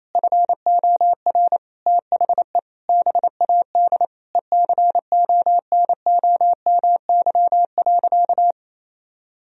Morse code, heard as a keyed tone, about 28 wpm, 700 Hz. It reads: FOR THE BAD ECONOMY.
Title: Sentences from Top 1000 Words - Review Part 3 - 28wpm